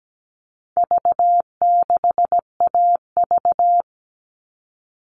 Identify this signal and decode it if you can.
Morse code, heard as a keyed tone, about 17 wpm, 700 Hz. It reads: V6AV
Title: All Cut Numbers - 4-Digit 17wpm